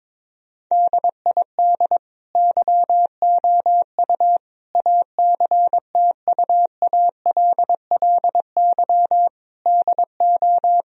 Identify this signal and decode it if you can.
Morse code, heard as a keyed tone, about 22 wpm, 700 Hz. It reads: DID YOU ACTUALLY DO
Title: Sentences from Top 1000 Words - Review Part 4 - 22wpm